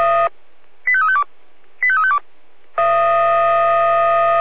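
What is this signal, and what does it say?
Five-tone selective call (ZVEI), heard as quick runs of five tones, tones of about 70 ms. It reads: 85352 85352
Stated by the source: ZVEI five-tone call received via SDR.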